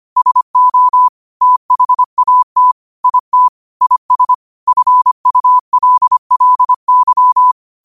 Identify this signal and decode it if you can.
Morse code, heard as a keyed tone, about 25 wpm, 1000 Hz. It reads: SO THAT IT IS FULLY
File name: qotdfiles / 25wpm